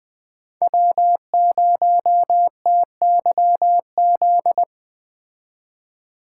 Morse code, heard as a keyed tone, about 20 wpm, 700 Hz. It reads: W0TYZ